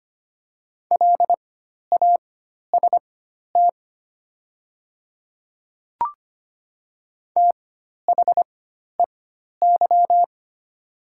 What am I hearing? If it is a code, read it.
Morse code, heard as a keyed tone, about 25 wpm, spaced as for 12 wpm, 700 Hz. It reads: LAST THEY
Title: 4x Character Spacing - Top 500 Words - 25wpm